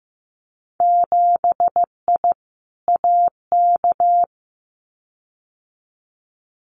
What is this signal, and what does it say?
Morse code, heard as a keyed tone, about 15 wpm, 700 Hz. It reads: 7I AK
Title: Winter Field Day Exchanges Encore 15wpm